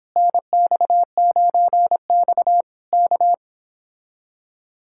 Morse code, heard as a keyed tone, about 26 wpm, 700 Hz. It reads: NX9X K
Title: Speed-Racing Common QSO Phrases Encore - 22wpm